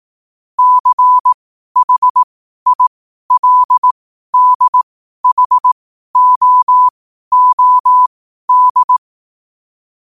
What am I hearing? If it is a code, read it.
Morse code, heard as a keyed tone, about 18 wpm, spaced as for 12 wpm, 1000 Hz. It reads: CHILDHOOD
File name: qotdfiles / 12wpm